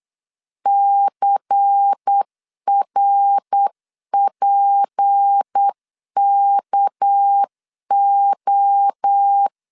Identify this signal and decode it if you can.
Morse code, heard as a keyed tone, about 8 wpm, 800 Hz. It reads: CRPKO
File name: CW / DET / 1 6 1500C S C 8wpm 5219